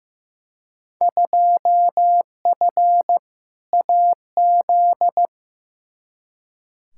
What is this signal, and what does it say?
Morse code, heard as a keyed tone, about 16 wpm, 700 Hz. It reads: 2F AZ
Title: Field Day Exchanges - 15wpm